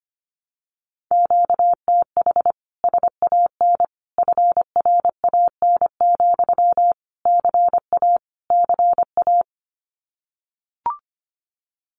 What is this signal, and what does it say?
Morse code, heard as a keyed tone, about 25 wpm, 700 Hz. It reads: QTH SAN FRAN, CA CA
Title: Common QSO Phrases Encore 25wpm